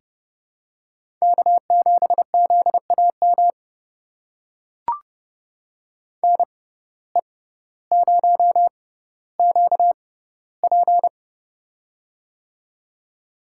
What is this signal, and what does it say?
Morse code, heard as a keyed tone, about 31 wpm, spaced as for 16 wpm, 700 Hz. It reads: K7ZAM NE0QP
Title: USA Call Signs - 5 Characters - 6x Character Spacing - 30wpm